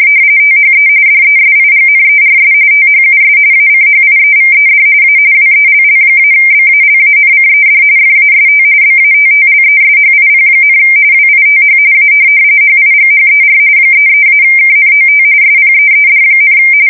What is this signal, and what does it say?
RTTY, about 45.45 baud. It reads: THE DESIGNERS THUS ACHIEVED A WELL-BALANCED FAST BATTLESHIP, WITH A HIGHER 33-KNOT SPEED,
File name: fsk170 / USS-New-Jersey